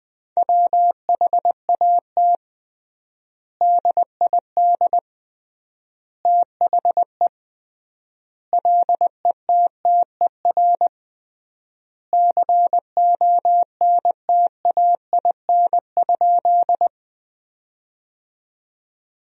Morse code, heard as a keyed tone, about 21 wpm, 700 Hz. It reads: WHAT DID THE LETTER CONTAIN?
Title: 3x Word-Spacing - Sentences from Top 500 Words - 20wpm